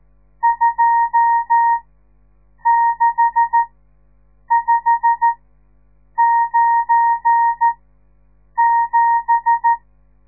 Morse code, heard as a keyed tone, about 14 wpm, spaced as for 7 wpm, 930 Hz. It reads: 26597
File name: radiomorningstar / morse / les 10